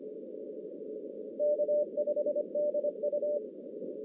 Morse code, heard as a keyed tone, about 25 wpm, 590 Hz. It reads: K5DU
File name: 2016 JAN CW NAQP / 2016 Jan naqp k5ot - 0-10